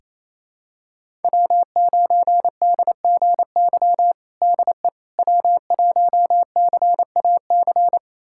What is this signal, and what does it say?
Morse code, heard as a keyed tone, about 28 wpm, 700 Hz. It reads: W9DGY DE W1CAC
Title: USA Callsign DE Callsign - 5 Characters - 28wpm